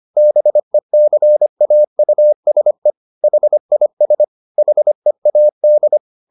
Morse code, heard as a keyed tone, about 25 wpm, 600 Hz. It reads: BECAUSE HIS HEAD